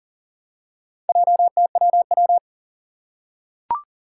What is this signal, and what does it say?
Morse code, heard as a keyed tone, about 39 wpm, 700 Hz. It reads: JTWW
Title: Four Challenging Letter-Number Combinations 40wpm